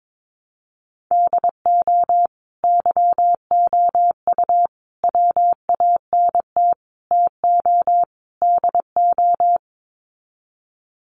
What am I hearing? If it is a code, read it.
Morse code, heard as a keyed tone, about 22 wpm, 700 Hz. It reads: DO YOU WANT TO DO
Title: Sets of 5 Words Encore 22wpm